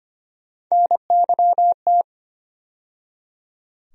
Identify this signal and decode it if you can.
Morse code, heard as a keyed tone, about 25 wpm, 700 Hz. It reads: NYT
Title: USA Call Sign Postfixes - 3 Letters 25wpm